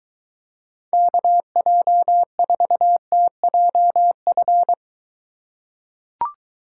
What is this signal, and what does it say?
Morse code, heard as a keyed tone, about 23 wpm, 700 Hz. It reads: KJ4TJF